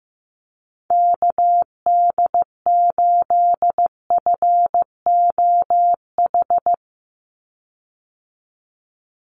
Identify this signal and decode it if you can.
Morse code, heard as a keyed tone, about 15 wpm, 700 Hz. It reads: KD8FOH
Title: USA Call Signs 15wpm